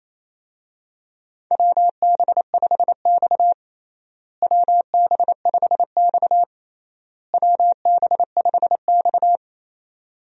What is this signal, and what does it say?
Morse code, heard as a keyed tone, about 28 wpm, 700 Hz. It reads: WB5X WB5X WB5X
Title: USA Call Signs - 4 Characters - Repeated 3x - 28wpm